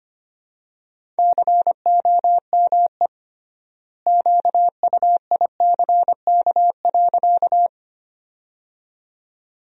Morse code, heard as a keyed tone, about 25 wpm, 700 Hz. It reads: COME QUICK.
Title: 3x Word-Spacing - Sentences from Top 500 Words - 25wpm